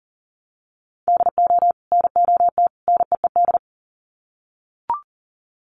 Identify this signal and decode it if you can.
Morse code, heard as a keyed tone, about 40 wpm, 700 Hz. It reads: DO NOT NEED
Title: Sets of 3 Words 40wpm